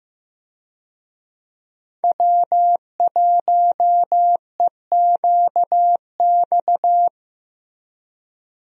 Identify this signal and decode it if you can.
Morse code, heard as a keyed tone, about 15 wpm, 700 Hz. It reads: W1EQX
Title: USA Call Signs - 5 Characters - Repeated 3x - 15wpm